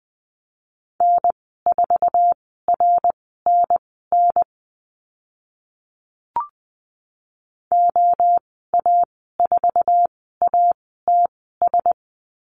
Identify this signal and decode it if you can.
Morse code, heard as a keyed tone, about 20 wpm, spaced as for 14 wpm, 700 Hz. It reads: N4RNN OA4ATS
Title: 082 Lesson 14g - Number 2 - Call Signs 20wpm